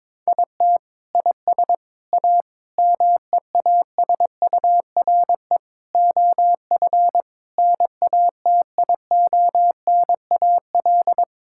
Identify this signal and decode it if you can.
Morse code, heard as a keyed tone, about 22 wpm, 700 Hz. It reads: IT IS A MEASURE OF NATIONAL
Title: Sentences from Top 1600 Words 22wpm